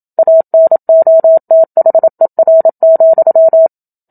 Morse code, heard as a keyed tone, about 27 wpm, 650 Hz. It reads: ANOTHER,
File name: Book Two / 01-Chapter 1 - 27 WPM